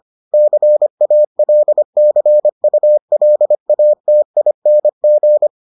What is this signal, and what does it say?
Morse code, heard as a keyed tone, about 25 wpm, 600 Hz. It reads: CALCULATING